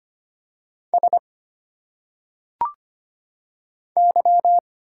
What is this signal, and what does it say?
Morse code, heard as a keyed tone, about 25 wpm, 700 Hz. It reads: S Y
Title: Single Letters - Phonetic Pronunciation 25wpm